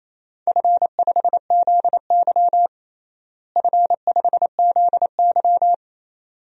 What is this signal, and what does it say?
Morse code, heard as a keyed tone, about 28 wpm, 700 Hz. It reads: F5ZY F5ZY
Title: DXCC-100 Call Signs - 4 Characters - Repeated 2x - 28wpm